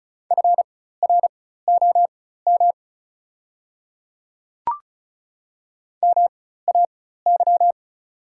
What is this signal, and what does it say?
Morse code, heard as a keyed tone, about 36 wpm, spaced as for 16 wpm, 700 Hz. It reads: FROM MAY
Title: 4x Character Spacing - Top 200 Words - 35wpm